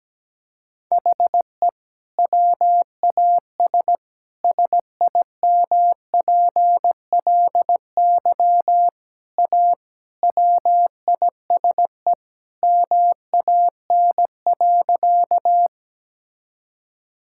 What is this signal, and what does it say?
Morse code, heard as a keyed tone, about 17 wpm, 700 Hz. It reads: HE WAS SIMPLY A WISE MAN.
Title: Sentences from Top 1600 Words 17wpm